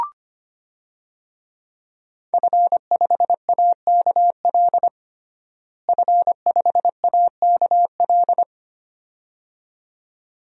Morse code, heard as a keyed tone, about 25 wpm, 700 Hz. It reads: F5AKL F5AKL
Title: DXCC-100 Call Signs - 5 Characters - Repeated 2x - 25wpm